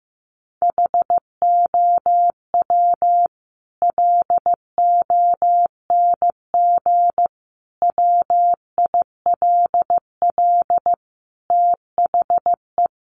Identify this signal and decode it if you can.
Morse code, heard as a keyed tone, about 15 wpm, 700 Hz. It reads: HOW LONG WILL THE
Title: Sentences from Top 1200 Words 15wpm